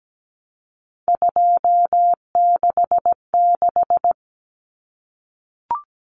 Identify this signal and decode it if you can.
Morse code, heard as a keyed tone, about 17 wpm, 700 Hz. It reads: 266
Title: Common Cut Numbers - 3-Digit 17wpm